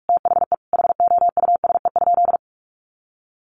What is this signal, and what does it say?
Morse code, heard as a keyed tone, about 46 wpm, 700 Hz. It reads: THE HOUSE?
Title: Sentences from Top 1100 Words 45wpm